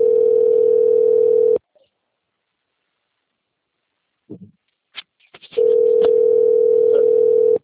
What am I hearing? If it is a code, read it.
Ringback tone (North America), 440 + 480 Hz.